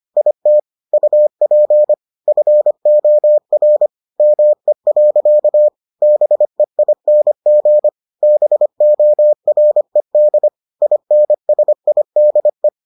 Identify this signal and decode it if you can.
Morse code, heard as a keyed tone, about 25 wpm, 600 Hz. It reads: IT UP FOR ME. BEING BORED INSIDE